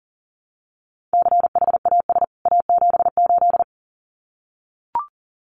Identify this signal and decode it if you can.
Morse code, heard as a keyed tone, about 40 wpm, 700 Hz. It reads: CHAS A78
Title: CWT Exchange 40wpm